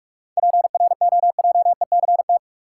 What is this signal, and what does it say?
Morse code, heard as a keyed tone, about 46 wpm, 700 Hz. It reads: PROJECT
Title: Sentences from Top 1400 Words - New Word Drill Alt - 45wpm